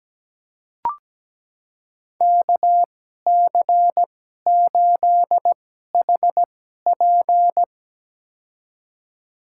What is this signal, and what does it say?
Morse code, heard as a keyed tone, about 17 wpm, spaced as for 12 wpm, 700 Hz. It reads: KC8HP
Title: USA Call Signs - 5 Characters - 2x Character Spacing - 17wpm